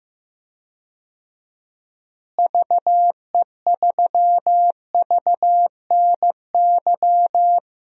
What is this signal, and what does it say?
Morse code, heard as a keyed tone, about 16 wpm, 700 Hz. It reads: VE3VNY
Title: DXCC-100 Call Signs - 6 Characters - Repeated 3x - 15wpm